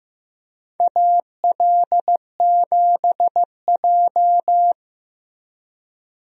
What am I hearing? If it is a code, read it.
Morse code, heard as a keyed tone, about 15 wpm, 700 Hz. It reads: AL7J
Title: USA Call Signs - 4 Characters - Repeated 2x - 15wpm